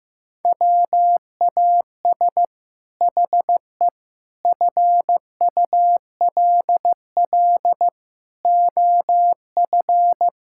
Morse code, heard as a keyed tone, about 15 wpm, 700 Hz. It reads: WAS HE FULL OF
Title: Sentences from Top 1000 Words - Review Part 6 - 15wpm